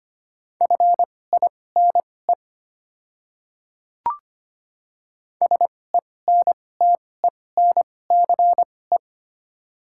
Morse code, heard as a keyed tone, about 25 wpm, spaced as for 18 wpm, 700 Hz. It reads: FINE SENTENCE
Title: 2x Character Spacing - Top 500 Words - 25wpm